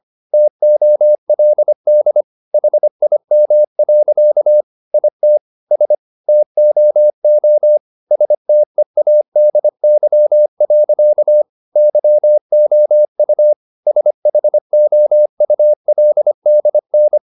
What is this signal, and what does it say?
Morse code, heard as a keyed tone, about 25 wpm, 600 Hz. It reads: TOLD HIM. IT S TOO STEADY. YOU SHOULDN
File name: the old man and the sea part 1 25wpm